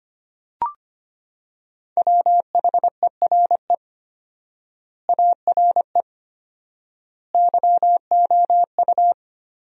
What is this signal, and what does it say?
Morse code, heard as a keyed tone, about 25 wpm, 700 Hz. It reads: WHERE ARE YOU
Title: 4x Word-Spacing - Sentences from Top 400 Words - 25wpm